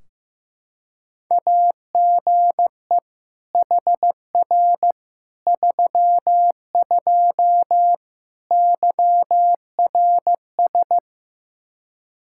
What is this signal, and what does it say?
Morse code, heard as a keyed tone, about 16 wpm, 700 Hz. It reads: AGE HR 32 YRS
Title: Practice QSO - Exchange 3 - More Info - 15wpm